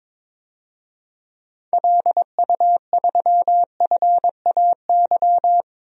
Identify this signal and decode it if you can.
Morse code, heard as a keyed tone, about 22 wpm, 700 Hz. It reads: LU3FAY
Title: DXCC-100 Call Signs - 6 Characters - 22wpm